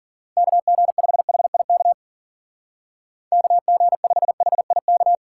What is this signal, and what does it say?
Morse code, heard as a keyed tone, about 48 wpm, 700 Hz. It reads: KG5HIK KG5HIK
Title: Speed Racing - USA Call Signs 40wpm